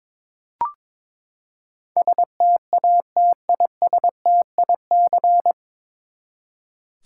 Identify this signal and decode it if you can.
Morse code, heard as a keyed tone, about 21 wpm, 700 Hz. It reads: STATISTIC
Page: Morse Code Ninja practice 22wpm